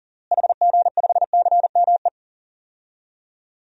Morse code, heard as a keyed tone, about 37 wpm, 700 Hz. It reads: HG5CME
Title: DXCC-100 Call Signs - 6 Characters - Repeated 3x - 40wpm